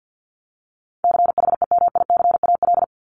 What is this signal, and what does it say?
Morse code, heard as a keyed tone, about 50 wpm, 700 Hz. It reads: CHEMICAL
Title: Top 1001-1250 Words 50wpm